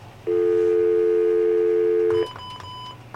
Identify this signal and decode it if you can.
Dial tone (North America), 350 + 440 Hz.